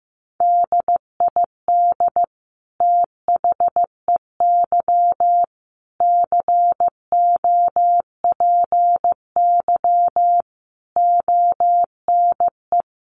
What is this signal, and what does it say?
Morse code, heard as a keyed tone, about 15 wpm, 700 Hz. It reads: DID THEY COPY ONE
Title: Sentences from Top 1400 Words 15wpm